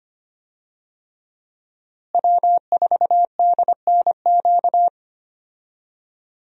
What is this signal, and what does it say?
Morse code, heard as a keyed tone, about 25 wpm, 700 Hz. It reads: W4DNQ
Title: USA Call Signs - 5 Characters - Repeated 3x - 25wpm